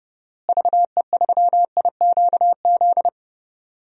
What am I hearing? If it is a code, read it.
Morse code, heard as a keyed tone, about 30 wpm, 700 Hz. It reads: VE3IQZ
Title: DXCC-100 Call Signs - 6 Characters - Repeated 3x - 30wpm